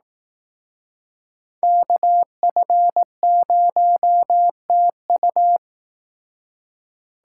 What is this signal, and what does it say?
Morse code, heard as a keyed tone, about 18 wpm, 700 Hz. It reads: KF0TU
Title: Speed Racing - USA 2x2 Call Signs 15wpm